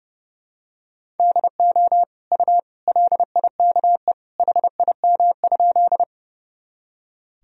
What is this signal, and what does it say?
Morse code, heard as a keyed tone, about 29 wpm, 700 Hz. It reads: DO U LIKE HIM?
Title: Sentences from Top 100 Words and Abbreviations 30wpm